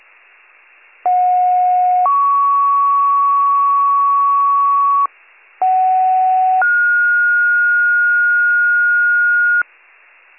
Two-tone page, heard as a one-second tone, then a three-second tone, about 730 then 1120 Hz, about 750 then 1430 Hz. It is heard from 1.1 to 9.6 s.